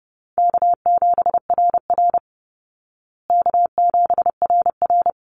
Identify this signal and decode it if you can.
Morse code, heard as a keyed tone, about 28 wpm, 700 Hz. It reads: K7RR K7RR
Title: Call Signs Repeated 3x - 30wpm